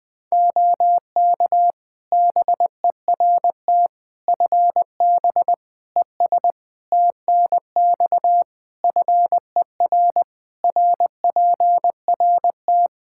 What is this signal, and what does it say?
Morse code, heard as a keyed tone, about 20 wpm, 700 Hz. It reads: OK BERT FB ES TNX FER RPRT
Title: Practice QSO - Exchange 2 - Info - 20wpm